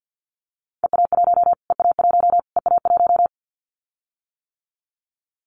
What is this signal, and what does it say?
Morse code, heard as a keyed tone, about 50 wpm, 700 Hz. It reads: EA1 EA1 EA1
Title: Callsign Prefixes for Top 25 DXCC - Repeated 3x - 50wpm